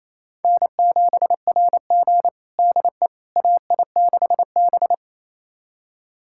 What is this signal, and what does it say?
Morse code, heard as a keyed tone, about 28 wpm, 700 Hz. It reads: N7RG DE AI6B
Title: USA Callsign DE Callsign - 4 Characters - 28wpm